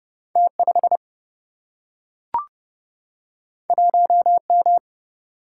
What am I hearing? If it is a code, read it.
Morse code, heard as a keyed tone, about 30 wpm, 700 Hz. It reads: T 5 1 M
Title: Single-Letter Single-Number 30wpm